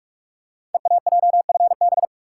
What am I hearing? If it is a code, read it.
Morse code, heard as a keyed tone, about 44 wpm, 700 Hz. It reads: EAJFB
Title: Code Groups - 5 Letter:Numbers - 45wpm